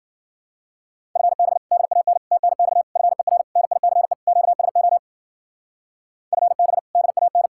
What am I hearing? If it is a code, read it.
Morse code, heard as a keyed tone, about 49 wpm, 700 Hz. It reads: FB DAN TNX FER NICE QSO FB DAN
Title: Speed-Racing Common QSO Phrases - 40wpm